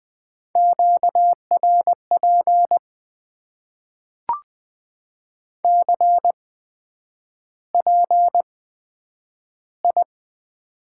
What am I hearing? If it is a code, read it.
Morse code, heard as a keyed tone, about 20 wpm, spaced as for 9 wpm, 700 Hz. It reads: QRP CPI
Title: Top 90 QSO Elements - 8x Character Spacing - 20wpm